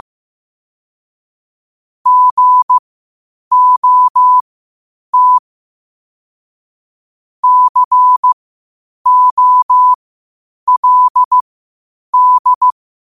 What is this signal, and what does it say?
Morse code, heard as a keyed tone, about 15 wpm, spaced as for 8 wpm, 1000 Hz. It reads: GOT COLD
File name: qotdfiles / 8wpm